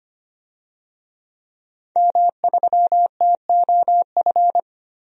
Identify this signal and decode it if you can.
Morse code, heard as a keyed tone, about 25 wpm, 700 Hz. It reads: M3TOF
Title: DXCC-100 Call Signs - 5 Characters - 25wpm